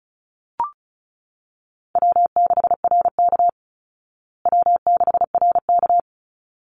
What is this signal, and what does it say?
Morse code, heard as a keyed tone, about 34 wpm, 700 Hz. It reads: W6RK W6RK
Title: Easy Call Signs Repeated 2x - 35wpm